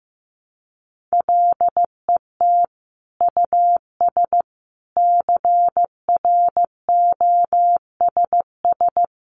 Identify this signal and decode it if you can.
Morse code, heard as a keyed tone, about 15 wpm, 700 Hz. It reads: LET US CROSS
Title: Sentences from Top 300 Words 15wpm